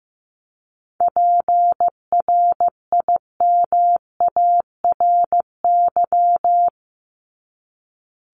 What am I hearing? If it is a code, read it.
Morse code, heard as a keyed tone, about 15 wpm, 700 Hz. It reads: PRIMARY
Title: LY Suffix - Top 100 Words with Base Words 15wpm